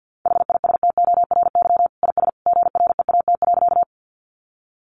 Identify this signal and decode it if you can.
Morse code, heard as a keyed tone, about 50 wpm, 700 Hz. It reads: HISTORY IS GREAT.
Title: Sentences from Top 600 Words 50wpm